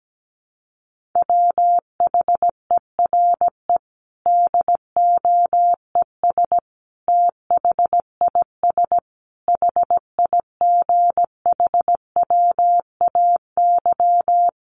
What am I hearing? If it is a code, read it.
Morse code, heard as a keyed tone, about 17 wpm, 700 Hz. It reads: WHERE DOES THIS HIGHWAY
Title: Sentences from Top 1400 Words 17wpm